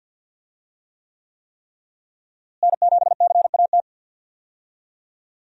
Morse code, heard as a keyed tone, about 48 wpm, 700 Hz. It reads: N7CAT
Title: USA Call Signs - 5 Characters - Repeated 3x - 50wpm